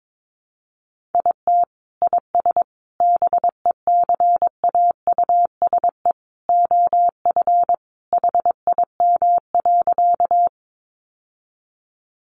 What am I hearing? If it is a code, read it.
Morse code, heard as a keyed tone, about 22 wpm, 700 Hz. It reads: IT IS BECAUSE OF HIM.
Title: Sentences from Top 600 Words 22wpm